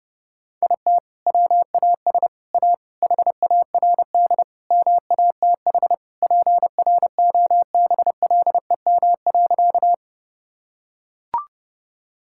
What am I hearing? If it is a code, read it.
Morse code, heard as a keyed tone, about 29 wpm, 700 Hz. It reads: IT WAS A HARD MATH PROBLEM.
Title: Sentences from Top 1000 Words - Review Part 6 - 30wpm